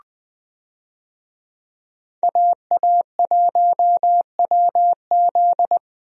Morse code, heard as a keyed tone, about 21 wpm, 700 Hz. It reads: AA1WZ
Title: USA Call Signs - 5 Characters - Repeated 2x - 20wpm